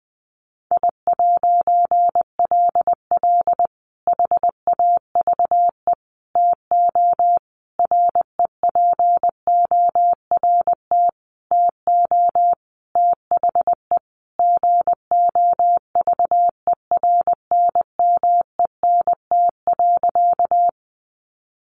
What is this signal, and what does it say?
Morse code, heard as a keyed tone, about 20 wpm, 700 Hz. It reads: I'LL HAVE TO REPORT TO THE GOVERNMENT.
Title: A Sound Of Thunder 20wpm